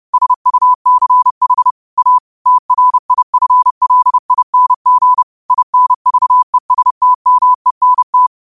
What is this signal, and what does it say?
Morse code, heard as a keyed tone, about 30 wpm, 1000 Hz. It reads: SUCH A TRIFLING INVESTMENT